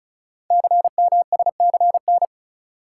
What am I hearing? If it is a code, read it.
Morse code, heard as a keyed tone, about 36 wpm, 700 Hz. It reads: CMSCN
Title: Code Groups - 5 Letters - 35wpm